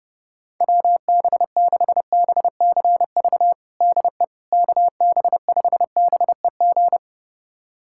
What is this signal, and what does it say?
Morse code, heard as a keyed tone, about 32 wpm, 700 Hz. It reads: WB6BCV DE KB5BEG
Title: USA Callsign DE Callsign - 6 Characters - 30wpm